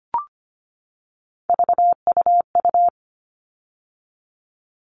Morse code, heard as a keyed tone, about 25 wpm, 700 Hz. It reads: VUU
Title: All Cut Numbers - 3-Digit 25wpm